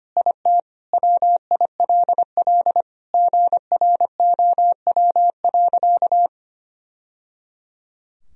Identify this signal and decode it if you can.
Morse code, heard as a keyed tone, about 25 wpm, 700 Hz. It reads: IT WILL GROW.
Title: Sentences from Top 1000 Words - Review Part 2 - 25wpm